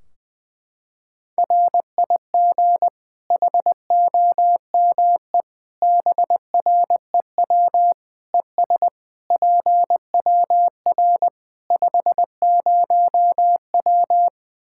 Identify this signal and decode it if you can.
Morse code, heard as a keyed tone, about 20 wpm, 700 Hz. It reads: RIG HOME BREW ES PWR 50W
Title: Practice QSO - Exchange 2 - Info - 20wpm